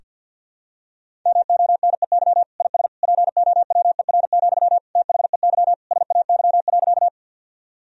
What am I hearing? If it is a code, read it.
Morse code, heard as a keyed tone, about 52 wpm, 700 Hz. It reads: MONEY IS POWER, THEY SAY.
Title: Sentences from Top 1000 Words - Review Part 2 - 50wpm